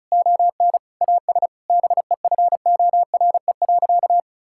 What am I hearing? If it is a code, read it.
Morse code, heard as a keyed tone, about 34 wpm, 700 Hz. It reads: ON AS BEFORE.
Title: Sentences from Top 1000 Words - Review Part 6 - 35wpm